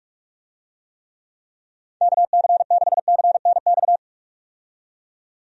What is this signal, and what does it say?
Morse code, heard as a keyed tone, about 46 wpm, 700 Hz. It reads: KC6CNX
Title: USA Call Signs - 6 Characters - 45wpm